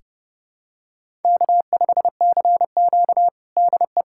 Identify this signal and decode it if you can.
Morse code, heard as a keyed tone, about 32 wpm, 700 Hz. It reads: K5CQ DE
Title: USA Callsign DE Callsign - 4 Characters - 30wpm